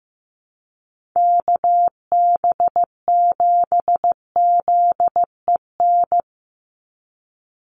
Morse code, heard as a keyed tone, about 15 wpm, 700 Hz. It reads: KB7ZEN
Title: USA Call Signs 15wpm